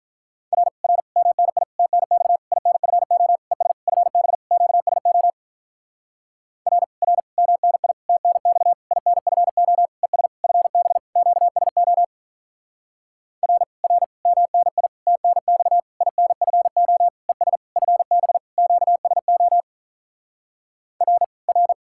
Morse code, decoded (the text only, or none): R R MNI TNX INFO ES FB QSO R R MNI TNX INFO ES FB QSO R R MNI TNX INFO ES FB QSO R R